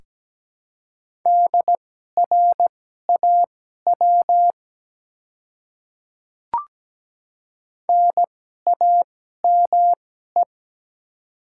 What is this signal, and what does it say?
Morse code, heard as a keyed tone, about 17 wpm, spaced as for 12 wpm, 700 Hz. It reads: DRAW NAME
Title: 2x Character Spacing - Top 300 Words - 17wpm